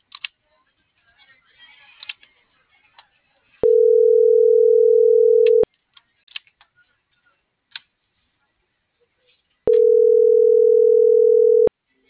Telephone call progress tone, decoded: ringback tone

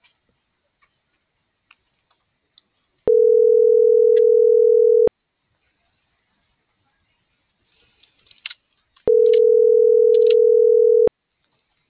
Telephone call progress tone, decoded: ringback tone

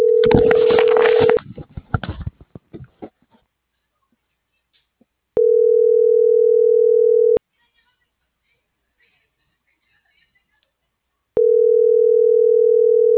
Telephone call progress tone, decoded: ringback tone

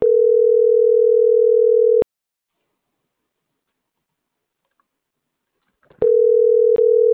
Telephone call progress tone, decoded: ringback tone

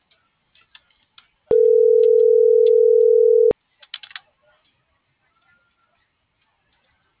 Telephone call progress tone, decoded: ringback tone

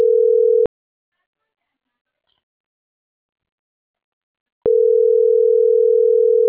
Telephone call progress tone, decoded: ringback tone

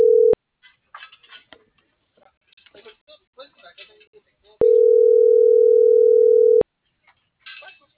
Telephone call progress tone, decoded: ringback tone

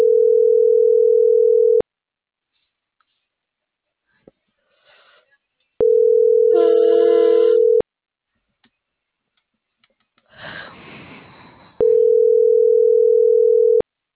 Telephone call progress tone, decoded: ringback tone

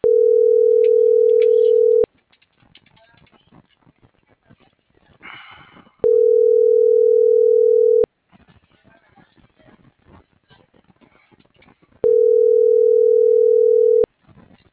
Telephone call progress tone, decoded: ringback tone

